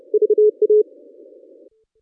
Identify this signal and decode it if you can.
Morse code, heard as a keyed tone, about 30 wpm, 440 Hz. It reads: VA
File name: contests / ss-cw-k5tr-op-n5rz -1200